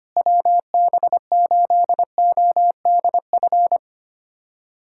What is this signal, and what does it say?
Morse code, heard as a keyed tone, about 25 wpm, 700 Hz. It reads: WB8ODF
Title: USA Callsign DE Callsign 25wpm